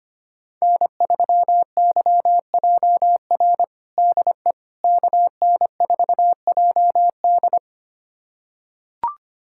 Morse code, heard as a keyed tone, about 25 wpm, 700 Hz. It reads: N3YJR DE KN4JD
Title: USA Callsign DE Callsign - 5 Characters - 25wpm